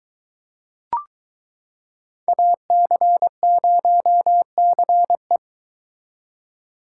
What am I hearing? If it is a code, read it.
Morse code, heard as a keyed tone, about 23 wpm, 700 Hz. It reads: AC0CE